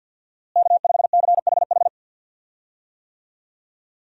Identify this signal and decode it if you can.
Morse code, heard as a keyed tone, about 49 wpm, 690 Hz. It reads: K5CHH
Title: USA Call Signs - 5 Characters - Repeated 3x - 50wpm